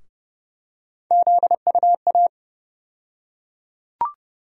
Morse code, heard as a keyed tone, about 30 wpm, 700 Hz. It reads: ZUA
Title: USA Call Sign Postfixes - 3 Letters 30wpm